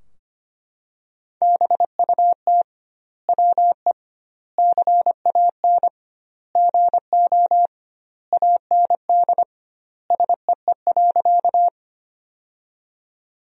Morse code, heard as a keyed tone, about 25 wpm, 700 Hz. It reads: BUT WE CAN GO AND SEE.
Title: 2x Word-Spacing - Sentences from Top 100 Words - 25wpm